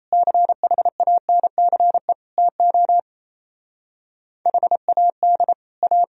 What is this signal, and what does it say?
Morse code, heard as a keyed tone, about 36 wpm, 700 Hz. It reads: CHANCE TO HAD A